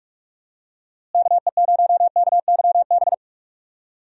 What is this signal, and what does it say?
Morse code, heard as a keyed tone, about 45 wpm, 700 Hz. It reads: KE0KYB
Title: USA Call Signs - 6 Characters - Repeated 3x - 45wpm